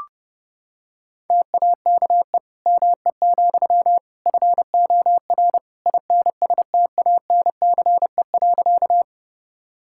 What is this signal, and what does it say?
Morse code, heard as a keyed tone, about 30 wpm, 700 Hz. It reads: TAKE ME, FOR INSTANCE.
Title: Sentences from Top 1000 Words - Review Part 5 - 30wpm